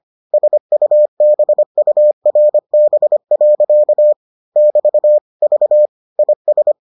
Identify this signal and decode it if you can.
Morse code, heard as a keyed tone, about 25 wpm, 600 Hz. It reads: SUBURB. <BT> V IS